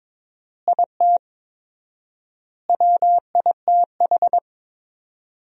Morse code, heard as a keyed tone, about 22 wpm, 700 Hz. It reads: IT WITH
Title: 4x Word-Spacing - Sentences from Top 500 Words - 22wpm